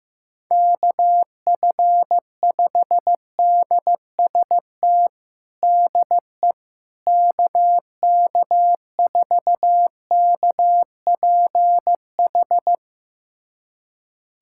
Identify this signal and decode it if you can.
Morse code, heard as a keyed tone, about 15 wpm, 700 Hz. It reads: KF5DST DE KK4KPH
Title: USA Callsign DE Callsign 15wpm